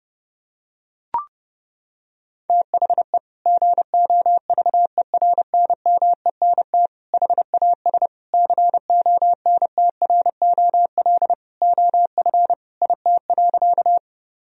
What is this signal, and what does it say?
Morse code, heard as a keyed tone, about 30 wpm, 700 Hz. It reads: THE GOVERNMENT HAS CONTROL OF IT.
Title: Sentences from Top 1000 Words - Review Part 3 - 30wpm